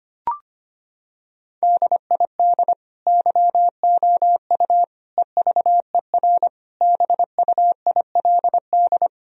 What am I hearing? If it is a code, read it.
Morse code, heard as a keyed tone, about 25 wpm, 700 Hz. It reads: DID YOU EVER BUILD
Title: Sentences from Top 1000 Words - Review Part 1 - 25wpm